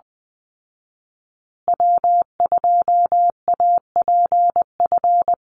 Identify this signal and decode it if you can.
Morse code, heard as a keyed tone, about 20 wpm, 700 Hz. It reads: W2APF
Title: Call Signs Repeated 2x - 20wpm